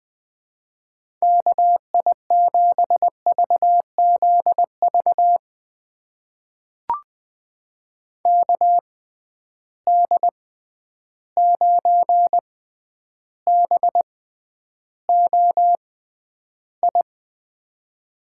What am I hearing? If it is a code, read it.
Morse code, heard as a keyed tone, about 20 wpm, spaced as for 10 wpm, 700 Hz. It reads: KI7VZV KD9BOI
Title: USA Call Signs - 6 Characters - 6x Character Spacing - 20wpm